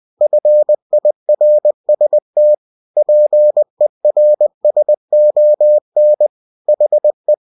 Morse code, heard as a keyed tone, about 20 wpm, 600 Hz. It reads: FIRST PERSON HE
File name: cw-books / winnie-the-pooh / winnie-20wpm-page-0004